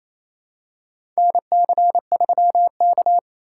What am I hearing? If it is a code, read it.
Morse code, heard as a keyed tone, about 28 wpm, 700 Hz. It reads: NC3K
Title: USA Call Signs - 4 Characters - 4x Character Spacing - 28wpm